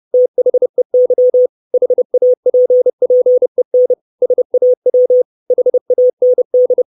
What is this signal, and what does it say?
Morse code, heard as a keyed tone, about 30 wpm, 500 Hz. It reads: THEY HAPPEN SAW HAND
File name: 632 Words 30WPM